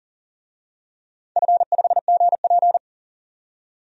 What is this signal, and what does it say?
Morse code, heard as a keyed tone, about 40 wpm, 700 Hz. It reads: F5GP
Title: DXCC-100 Call Signs - 4 Characters - 40wpm